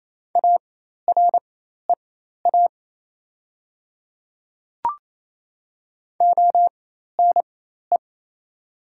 Morse code, heard as a keyed tone, about 28 wpm, spaced as for 13 wpm, 690 Hz. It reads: AREA ONE
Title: 4x Character Spacing - Top 500 Words - 28wpm